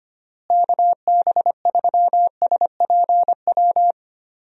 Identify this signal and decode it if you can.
Morse code, heard as a keyed tone, about 25 wpm, 700 Hz. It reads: KB3SPW
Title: USA Call Signs - 6 Characters - Repeated 3x - 25wpm